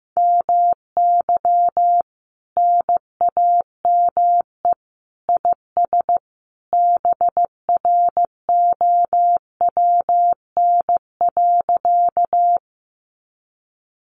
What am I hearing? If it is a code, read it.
Morse code, heard as a keyed tone, about 15 wpm, 700 Hz. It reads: MY NAME IS BROWN.
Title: Sentences from Top 1200 Words 15wpm